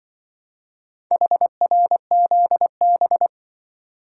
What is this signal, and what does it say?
Morse code, heard as a keyed tone, about 25 wpm, 700 Hz. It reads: HRZB
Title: Speed-Racing - Four Letter-Number Combinations - 20wpm